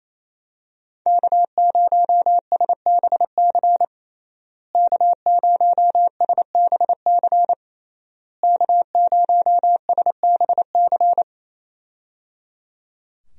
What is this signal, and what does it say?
Morse code, heard as a keyed tone, about 27 wpm, 700 Hz. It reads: K0SBC K0SBC K0SBC
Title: USA Call Signs - 5 Characters - Repeated 3x - 28wpm